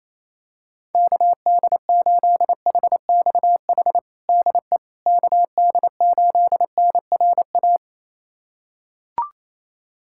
Morse code, heard as a keyed tone, about 28 wpm, 700 Hz. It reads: KD8HXH DE KD8NRA
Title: USA Callsign DE Callsign 28wpm